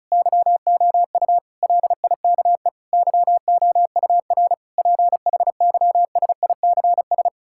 Morse code, heard as a keyed tone, about 36 wpm, 700 Hz. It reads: YOU LIKE YOUR PHYSICS
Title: Sentences from Top 1000 Words - Review Part 5 - 35wpm